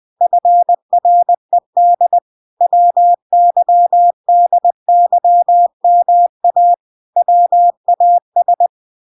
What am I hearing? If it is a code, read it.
Morse code, heard as a keyed tone, about 20 wpm, 700 Hz. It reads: FRED WYDYMA WAS